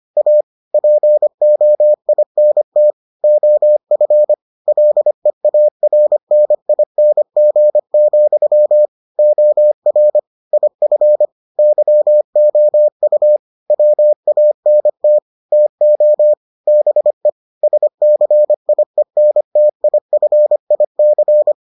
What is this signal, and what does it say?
Morse code, heard as a keyed tone, about 25 wpm, 600 Hz. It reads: A POINT OF LEARNING, OR IF YOU WANT TO BE SCIENTIFIC